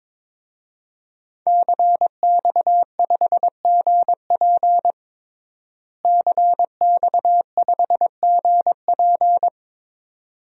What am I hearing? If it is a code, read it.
Morse code, heard as a keyed tone, about 22 wpm, 700 Hz. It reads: CX5GP CX5GP
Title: DXCC-100 Call Signs - 5 Characters - Repeated 2x - 22wpm